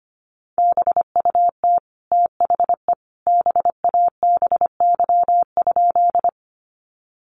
Morse code, heard as a keyed tone, about 25 wpm, 700 Hz. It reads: BUT THE BABY?
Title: Sentences from Top 1300 Words 25wpm